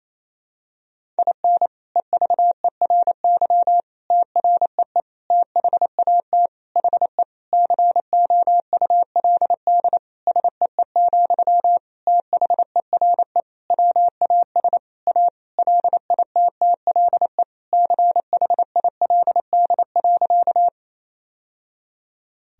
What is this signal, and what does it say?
Morse code, heard as a keyed tone, about 28 wpm, 700 Hz. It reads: IN EVERY TREE THAT HE COULD SEE, THERE WAS A LITTLE CHILD.
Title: Sentences from Top 1000 Words - Review Part 1 - 28wpm